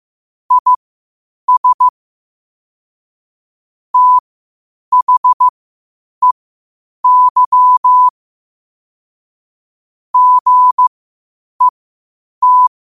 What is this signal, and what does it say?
Morse code, heard as a keyed tone, about 15 wpm, spaced as for 8 wpm, 1000 Hz. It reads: IS THEY GET
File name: qotdfiles / 8wpm